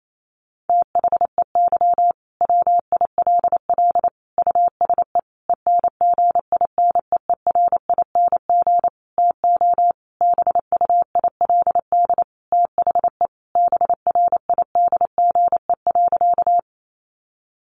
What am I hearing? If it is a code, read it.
Morse code, heard as a keyed tone, about 28 wpm, 700 Hz. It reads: THEY WILL USE ENGINEERING TO BUILD THE BRIDGE.
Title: Sentences from Top 1500 Words 28wpm